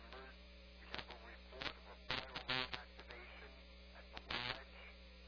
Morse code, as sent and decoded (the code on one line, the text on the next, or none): none